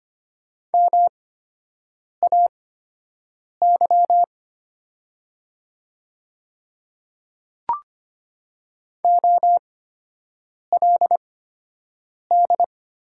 Morse code, heard as a keyed tone, about 25 wpm, spaced as for 7 wpm, 700 Hz. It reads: MAY OLD
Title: 8x Character Spacing - Top 300 Words - 25wpm